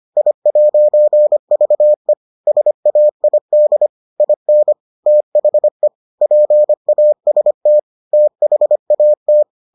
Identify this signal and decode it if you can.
Morse code, heard as a keyed tone, about 25 wpm, 600 Hz. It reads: I'VE SAID IN THE PAST THAT